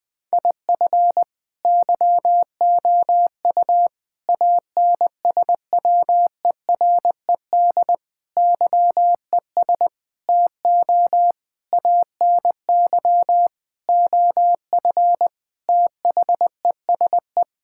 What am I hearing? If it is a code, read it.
Morse code, heard as a keyed tone, about 20 wpm, 700 Hz. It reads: IF YOU ANSWERED YES TO ANY OF THESE
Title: Sets of 9 Words - 20wpm